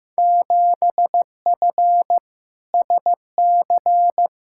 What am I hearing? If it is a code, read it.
Morse code, heard as a keyed tone, about 15 wpm, 700 Hz. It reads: 7F SC
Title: Field Day Exchanges Encore - 15wpm